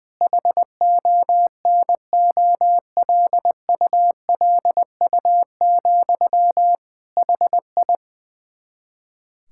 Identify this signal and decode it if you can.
Morse code, heard as a keyed tone, about 20 wpm, 700 Hz. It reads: HONOLULU, HI
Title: K1USN SST QSO Practice 20wpm